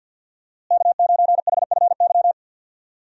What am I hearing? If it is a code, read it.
Morse code, heard as a keyed tone, about 51 wpm, 700 Hz. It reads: K9HFY
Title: USA Call Signs - 5 Characters - Repeated 2x - 50wpm